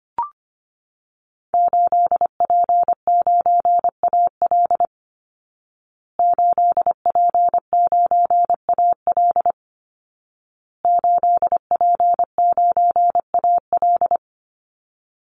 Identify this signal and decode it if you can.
Morse code, heard as a keyed tone, about 25 wpm, 700 Hz. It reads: 8P9AL 8P9AL 8P9AL
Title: Call Signs Repeated 2x - 25wpm